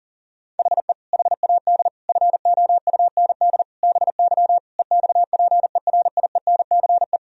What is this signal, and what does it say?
Morse code, heard as a keyed tone, about 40 wpm, 700 Hz. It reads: HE HAD FOUND BY EXPERIENCE
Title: Sentences from Top 1000 Words - Review Part 3 - 40wpm